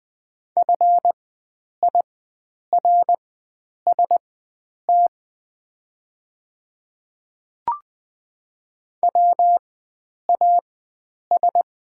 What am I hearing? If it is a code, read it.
Morse code, heard as a keyed tone, about 20 wpm, spaced as for 9 wpm, 700 Hz. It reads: FIRST WAS
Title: 4x Character Spacing - Top 100 Words - 20wpm